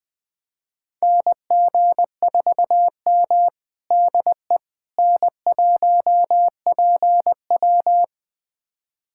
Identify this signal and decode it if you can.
Morse code, heard as a keyed tone, about 21 wpm, 700 Hz. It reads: NG4M DE N1PW
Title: USA Callsign DE Callsign - 4 Characters - 20wpm